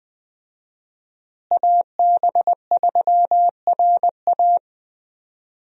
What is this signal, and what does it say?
Morse code, heard as a keyed tone, about 20 wpm, 700 Hz. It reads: AB3RA
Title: USA Call Signs - 5 Characters - Repeated 3x - 20wpm